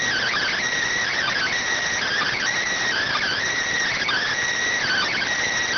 Emergency Alert System (SAME) tones, heard as harsh buzzing bursts. It is heard from 1.1 to 5.3 s.
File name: findings / CT AXIAL -572mm 2.gen-38 rx